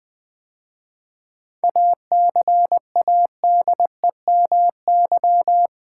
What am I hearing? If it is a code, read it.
Morse code, heard as a keyed tone, about 20 wpm, 700 Hz. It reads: ACADEMY
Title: 7 Letter Words - First 3 Letters Given - 20wpm